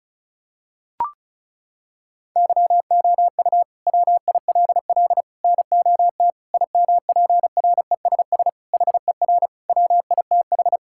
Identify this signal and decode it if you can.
Morse code, heard as a keyed tone, about 36 wpm, 700 Hz. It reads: YOU WILL NOT IMPRESS HER WITH
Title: Sentences from Top 1600 Words 35wpm